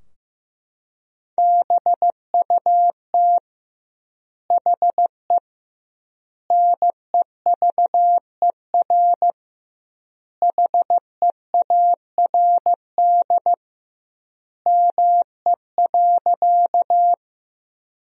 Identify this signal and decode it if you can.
Morse code, heard as a keyed tone, about 15 wpm, 700 Hz. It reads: BUT HE NEVER HEARD ME.
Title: 2x Word-Spacing - Sentences from Top 400 Words - 15wpm